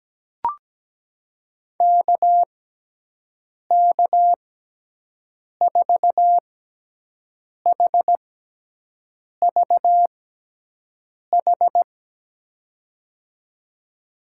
Morse code, heard as a keyed tone, about 17 wpm, spaced as for 6 wpm, 700 Hz. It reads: KK4HVH
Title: USA Call Signs - 6x Character Spacing - 17wpm